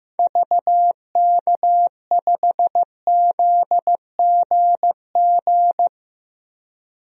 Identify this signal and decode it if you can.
Morse code, heard as a keyed tone, about 15 wpm, 700 Hz. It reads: VK5ZGG
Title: DXCC-100 Call Signs - 6 Characters - 15wpm